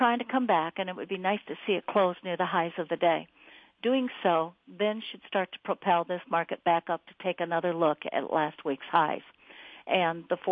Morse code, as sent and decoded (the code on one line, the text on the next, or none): none